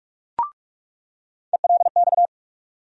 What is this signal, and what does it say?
Morse code, heard as a keyed tone, about 46 wpm, 700 Hz. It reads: ELX
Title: USA Call Sign Postfixes - 3 Letters 45wpm